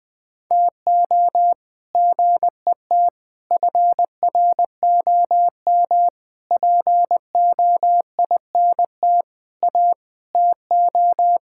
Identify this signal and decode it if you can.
Morse code, heard as a keyed tone, about 20 wpm, 700 Hz. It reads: TO GET FROM POINT A TO
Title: Sets of 8 Words - 20wpm